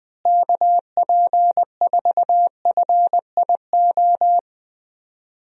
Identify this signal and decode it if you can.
Morse code, heard as a keyed tone, about 20 wpm, 700 Hz. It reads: KP4FIO
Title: USA Call Signs - 6 Characters - Repeated 3x - 20wpm